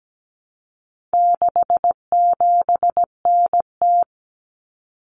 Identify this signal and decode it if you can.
Morse code, heard as a keyed tone, about 17 wpm, 700 Hz. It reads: 67NT
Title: Common Cut Numbers - 4-Digit 17wpm